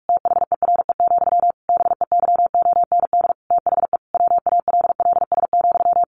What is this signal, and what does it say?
Morse code, heard as a keyed tone, about 44 wpm, 700 Hz. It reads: THERE, BEYOND THE WALLS,
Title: A Sound Of Thunder 45wpm